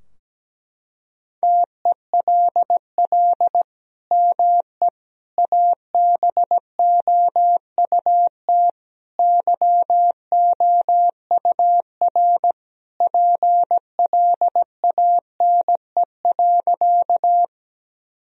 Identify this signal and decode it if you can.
Morse code, heard as a keyed tone, about 17 wpm, 700 Hz. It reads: TELL ME ABOUT YOUR PLANE.
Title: Sentences from Top 1000 Words - Review Part 3 - 17wpm